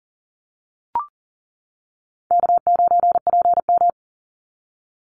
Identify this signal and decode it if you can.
Morse code, heard as a keyed tone, about 40 wpm, 700 Hz. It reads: K9PM
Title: NA SOTA Activator Callsigns 40wpm